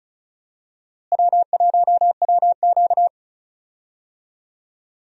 Morse code, heard as a keyed tone, about 36 wpm, 700 Hz. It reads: W1WQ
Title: USA Call Signs - 4 Characters - Repeated 2x - 35wpm